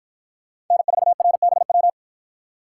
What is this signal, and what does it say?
Morse code, heard as a keyed tone, about 53 wpm, 700 Hz. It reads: N4RBW